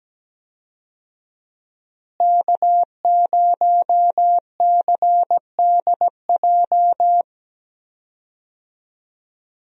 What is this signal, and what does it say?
Morse code, heard as a keyed tone, about 17 wpm, 700 Hz. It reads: K0CDJ
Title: USA Call Signs - 5 Characters - Repeated 2x - 17wpm